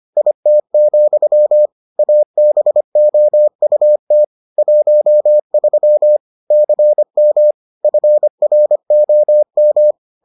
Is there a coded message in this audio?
Yes: Morse code from 0.2 to 9.9 s, heard as a keyed tone, about 25 wpm, 600 Hz. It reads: IT, ABOUT 13 CM FROM